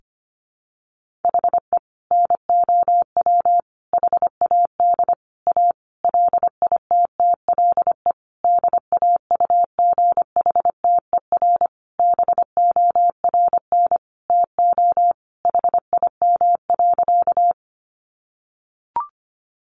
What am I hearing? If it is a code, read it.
Morse code, heard as a keyed tone, about 25 wpm, 700 Hz. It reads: HE NOW HAD A LITTLE DAUGHTER BORN TO HIM.
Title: Sentences from Top 1500 Words 25wpm